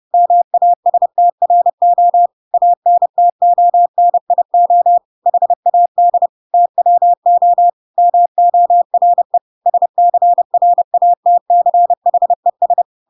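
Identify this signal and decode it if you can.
Morse code, heard as a keyed tone, about 30 wpm, 700 Hz. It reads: MASTRO ANTONIO HAD TWO MORE SCRATCHES
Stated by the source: Chapter 2 of The Adventures of Pinocchio read as a book in Morse code (CW) at 30 wpm, by ik1hge.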